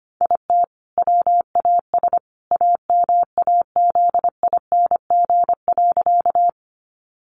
Morse code, heard as a keyed tone, about 25 wpm, 700 Hz. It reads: IT WAS AMAZING.
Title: Sentences from Top 1500 Words 25wpm